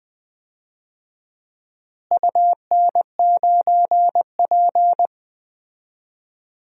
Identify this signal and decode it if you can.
Morse code, heard as a keyed tone, about 21 wpm, 700 Hz. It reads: UN9P
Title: DXCC-100 Call Signs - 4 Characters - Repeated 2x - 20wpm